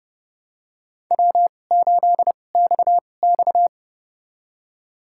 Morse code, heard as a keyed tone, about 30 wpm, spaced as for 22 wpm, 700 Hz. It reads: W8XX
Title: USA Call Signs - 4 Characters - 2x Character Spacing - 30wpm